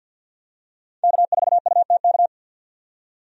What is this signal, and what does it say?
Morse code, heard as a keyed tone, about 47 wpm, 700 Hz. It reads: K4UTK
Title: USA Call Signs - 5 Characters - 50wpm